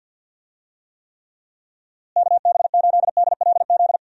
Morse code, heard as a keyed tone, about 52 wpm, 700 Hz. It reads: KB8DLZ
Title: USA Call Signs - 6 Characters - Repeated 3x - 50wpm